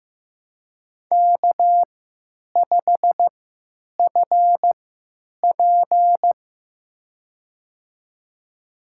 Morse code, heard as a keyed tone, about 15 wpm, spaced as for 9 wpm, 700 Hz. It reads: K5FP
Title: USA Call Signs - 4 Characters - 3x Character Spacing - 15wpm